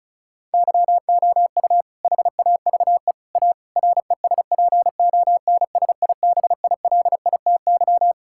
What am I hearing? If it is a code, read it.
Morse code, heard as a keyed tone, about 36 wpm, 700 Hz. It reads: YOU HAVE A RESPONSIBILITY
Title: Sentences from Top 1600 Words 35wpm